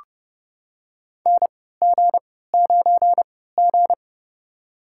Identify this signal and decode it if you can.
Morse code, heard as a keyed tone, about 30 wpm, spaced as for 17 wpm, 700 Hz. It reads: NG9G
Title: USA Call Signs - 4 Characters - 3x Character Spacing - 30wpm